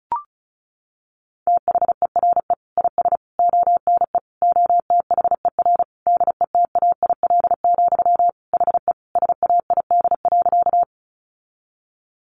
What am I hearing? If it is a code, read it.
Morse code, heard as a keyed tone, about 34 wpm, 700 Hz. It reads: THERE IS ONE OTHER DETAIL, HE SAID.
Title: Sentences from Top 1500 Words 35wpm